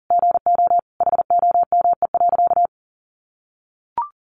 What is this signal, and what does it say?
Morse code, heard as a keyed tone, about 40 wpm, 700 Hz. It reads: GO HOME.
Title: Sentences from Top 1400 Words 40wpm